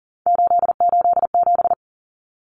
Morse code, heard as a keyed tone, about 40 wpm, 700 Hz. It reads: 887
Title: Numbers - 3-Digit 40wpm